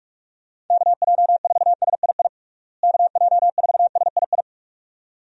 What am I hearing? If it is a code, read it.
Morse code, heard as a keyed tone, about 41 wpm, 700 Hz. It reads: KJ4SII KJ4SII
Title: USA Call Signs - 6 Characters - Repeated 2x - 45wpm